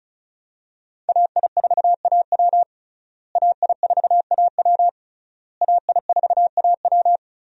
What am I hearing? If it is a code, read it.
Morse code, heard as a keyed tone, about 36 wpm, 700 Hz. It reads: AI4AW AI4AW AI4AW
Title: USA Call Signs - 5 Characters - Repeated 3x - 35wpm